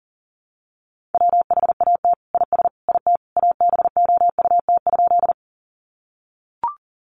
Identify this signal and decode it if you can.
Morse code, heard as a keyed tone, about 40 wpm, 700 Hz. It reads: WHAT IS IT ABOUT?
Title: Sentences from Top 100 Words 40wpm